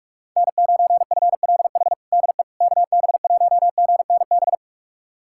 Morse code, heard as a keyed tone, about 41 wpm, 700 Hz. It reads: N9FLH DE KB1GNB
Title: USA Callsign DE Callsign 45wpm